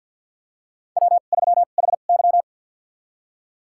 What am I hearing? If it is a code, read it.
Morse code, heard as a keyed tone, about 51 wpm, spaced as for 36 wpm, 700 Hz. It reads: W3HY
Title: USA Call Signs - 4 Characters - 2x Character Spacing - 50wpm